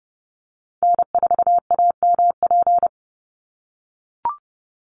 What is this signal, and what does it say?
Morse code, heard as a keyed tone, about 30 wpm, 700 Hz. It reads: N4AMP
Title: USA Call Signs 30wpm